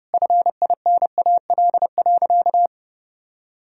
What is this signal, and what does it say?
Morse code, heard as a keyed tone, about 32 wpm, 700 Hz. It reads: FINAL.
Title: Sentences from Top 1000 Words - Review Part 2 - 30wpm